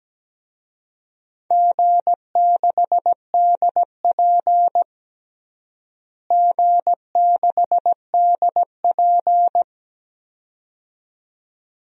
Morse code, heard as a keyed tone, about 17 wpm, 700 Hz. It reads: G6DP G6DP
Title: DXCC-100 Call Signs - 4 Characters - Repeated 2x - 17wpm